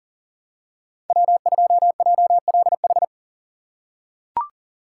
Morse code, heard as a keyed tone, about 40 wpm, 700 Hz. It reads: W2JLH
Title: USA Call Signs - 5 Characters - 2x Character Spacing - 40wpm